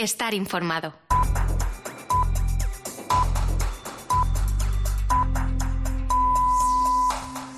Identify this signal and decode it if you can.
Radio time signal (time pips), with speech. It is heard from 1.1 to 7.1 s.